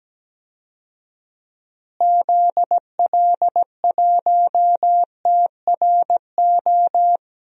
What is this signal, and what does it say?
Morse code, heard as a keyed tone, about 17 wpm, 700 Hz. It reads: ZL1TRO
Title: DXCC-100 Call Signs - 6 Characters - Repeated 2x - 17wpm